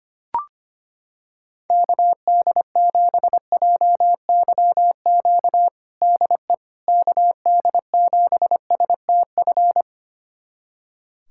Morse code, heard as a keyed tone, about 25 wpm, 700 Hz. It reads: KD7JYQ DE KD7STF
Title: USA Callsign DE Callsign 25wpm